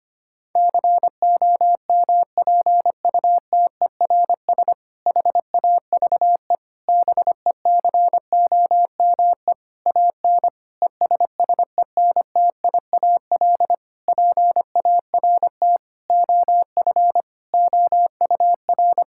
Morse code, heard as a keyed tone, about 25 wpm, 700 Hz. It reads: COMPUTERS HAVE BECOME AN ESSENTIAL PART OF OUR
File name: practice / 25wpm / Difficult Sentences with Hints - 25wpm